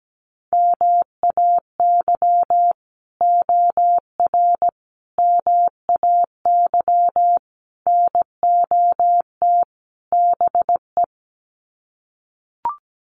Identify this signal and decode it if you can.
Morse code, heard as a keyed tone, about 17 wpm, 700 Hz. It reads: MAY OR MAY NOT BE
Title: Sets of 5 Words 17wpm